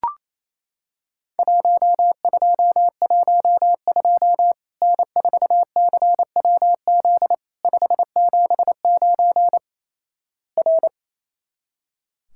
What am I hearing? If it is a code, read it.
Morse code, heard as a keyed tone, about 28 wpm, 700 Hz. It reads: 1212 N4CWZ 579 R
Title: Worked All Europe DX Contest - QTC Exchange - 28wpm